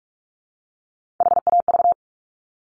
Morse code, heard as a keyed tone, about 46 wpm, 690 Hz. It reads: HAV
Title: Top Three Letter Combinations in Words 45wpm